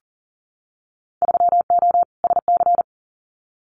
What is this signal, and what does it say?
Morse code, heard as a keyed tone, about 38 wpm, 700 Hz. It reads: 3O SC
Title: Winter Field Day Exchanges Encore 40wpm